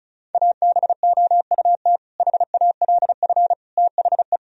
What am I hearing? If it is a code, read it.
Morse code, heard as a keyed tone, about 36 wpm, 700 Hz. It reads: ABOUT HALF THE
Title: Sentences from Top 1000 Words - Review Part 1 - 35wpm